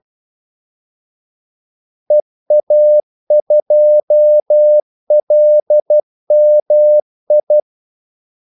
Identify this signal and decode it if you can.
Morse code, heard as a keyed tone, about 12 wpm, 600 Hz. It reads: EA2LMI
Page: Kaldesignaler 12 wpm